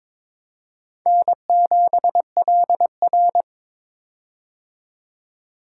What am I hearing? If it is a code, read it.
Morse code, heard as a keyed tone, about 22 wpm, 700 Hz. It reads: N7LR
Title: K1USN SST QSO Practice - 22wpm